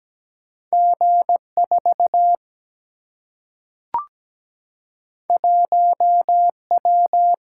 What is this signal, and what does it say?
Morse code, heard as a keyed tone, about 17 wpm, 700 Hz. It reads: G4 1W
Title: Single-Letter Single-Number 17wpm